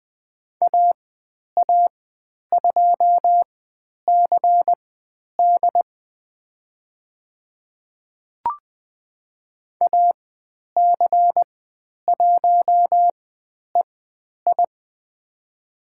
Morse code, heard as a keyed tone, about 20 wpm, spaced as for 10 wpm, 700 Hz. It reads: AA2CD AC1EI
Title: 109 Lesson 18g - Letter m - Call Signs 20wpm